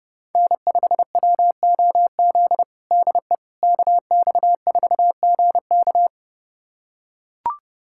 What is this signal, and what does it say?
Morse code, heard as a keyed tone, about 30 wpm, 700 Hz. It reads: N5WOZ DE KX4GK
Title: USA Callsign DE Callsign - 5 Characters - 30wpm